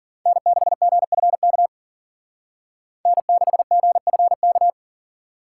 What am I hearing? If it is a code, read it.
Morse code, heard as a keyed tone, about 50 wpm, 700 Hz. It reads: N6GFK N6GFK
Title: Speed Racing - USA 1x3 Call Signs 40wpm